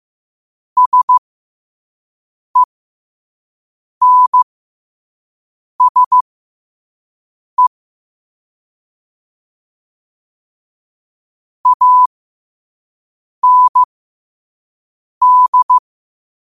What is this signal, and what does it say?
Morse code, heard as a keyed tone, about 15 wpm, spaced as for 5 wpm, 1000 Hz. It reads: SENSE AND